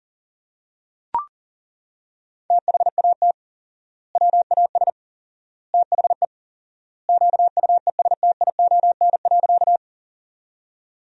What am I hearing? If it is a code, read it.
Morse code, heard as a keyed tone, about 37 wpm, 700 Hz. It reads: THAT WAS THE QUESTION.
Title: 4x Word-Spacing - Sentences from Top 400 Words - 40wpm